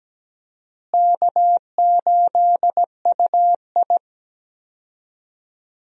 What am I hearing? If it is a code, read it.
Morse code, heard as a keyed tone, about 17 wpm, 700 Hz. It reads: K8UI
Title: USA Call Signs - 4 Characters - Repeated 3x - 17wpm